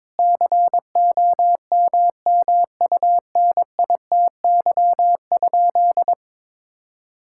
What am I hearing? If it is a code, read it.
Morse code, heard as a keyed tone, about 22 wpm, 700 Hz. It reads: COMMUNITY?
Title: Sentences from Top 1000 Words - Review Part 5 - 22wpm